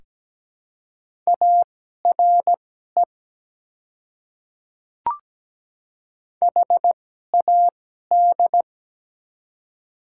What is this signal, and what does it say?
Morse code, heard as a keyed tone, about 17 wpm, spaced as for 12 wpm, 700 Hz. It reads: ARE HAD
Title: 2x Character Spacing - Top 100 Words - 17wpm